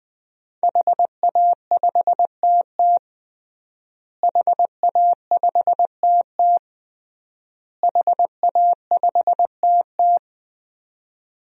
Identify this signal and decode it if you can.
Morse code, heard as a keyed tone, about 20 wpm, 700 Hz. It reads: HA5TT HA5TT HA5TT
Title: DXCC-100 Call Signs - 5 Characters - Repeated 3x - 20wpm